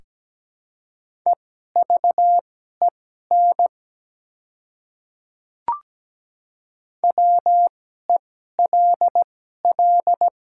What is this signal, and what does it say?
Morse code, heard as a keyed tone, about 17 wpm, spaced as for 12 wpm, 700 Hz. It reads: EVEN WELL
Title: 2x Character Spacing - Top 200 Words - 17wpm